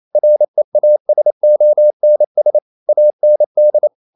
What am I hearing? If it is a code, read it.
Morse code, heard as a keyed tone, about 28 wpm, 600 Hz. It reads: REASONS AND
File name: Never 0027wpm